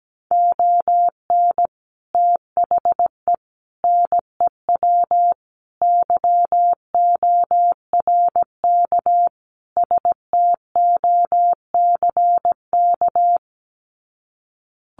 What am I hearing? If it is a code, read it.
Morse code, heard as a keyed tone, about 17 wpm, 700 Hz. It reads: ON THE NEW YORK STOCK
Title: Sets of 5 Words Encore 17wpm